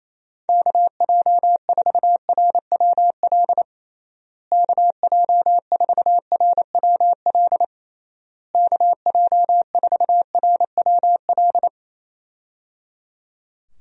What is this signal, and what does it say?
Morse code, heard as a keyed tone, about 28 wpm, 700 Hz. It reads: KJ4RWL KJ4RWL KJ4RWL
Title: USA Call Signs - 6 Characters - Repeated 3x - 28wpm